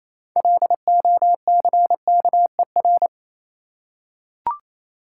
Morse code, heard as a keyed tone, about 28 wpm, 700 Hz. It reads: LOCKER